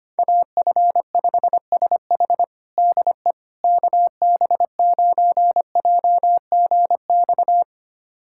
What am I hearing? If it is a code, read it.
Morse code, heard as a keyed tone, about 25 wpm, 700 Hz. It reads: AF5SH DE KB9JGX
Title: USA Callsign DE Callsign 25wpm